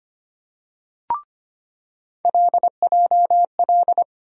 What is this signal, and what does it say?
Morse code, heard as a keyed tone, about 25 wpm, 700 Hz. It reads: LJL
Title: USA Call Sign Postfixes - 3 Letters 25wpm